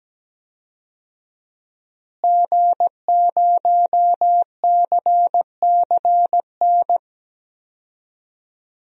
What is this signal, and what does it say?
Morse code, heard as a keyed tone, about 17 wpm, 700 Hz. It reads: G0CCN
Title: DXCC-100 Call Signs - 5 Characters - Repeated 3x - 17wpm